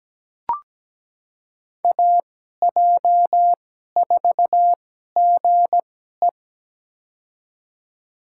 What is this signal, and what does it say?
Morse code, heard as a keyed tone, about 17 wpm, spaced as for 12 wpm, 700 Hz. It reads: AJ4GE
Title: USA Call Signs - 5 Characters - 2x Character Spacing - 17wpm